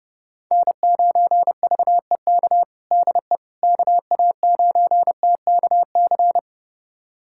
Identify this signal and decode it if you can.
Morse code, heard as a keyed tone, about 30 wpm, 700 Hz. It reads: N9VEK DE KA9TKC
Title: USA Callsign DE Callsign 30wpm